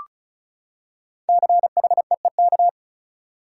Morse code, heard as a keyed tone, about 36 wpm, 700 Hz. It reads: CHEEK
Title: Five Letter Words - 35wpm